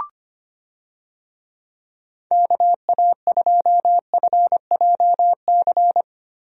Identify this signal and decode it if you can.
Morse code, heard as a keyed tone, about 25 wpm, 700 Hz. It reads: KA2FJC
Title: USA Call Signs - 6 Characters - Repeated 2x - 25wpm